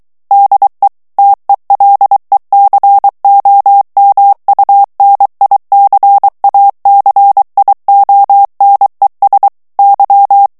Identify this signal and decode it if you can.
Morse code, heard as a keyed tone, about 23 wpm, 800 Hz. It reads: DE TELECOMUNICACIONES Y
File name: morse mp3 / ESPAGNOL TEXTE EN CLAIR 2 VITESSE 22 MOTS MINUTES